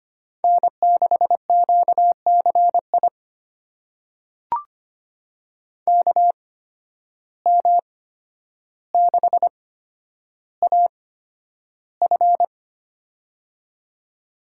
Morse code, heard as a keyed tone, about 25 wpm, spaced as for 11 wpm, 700 Hz. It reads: N6QCI KM6AF
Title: USA Call Signs - 5 Characters - 8x Character Spacing - 25wpm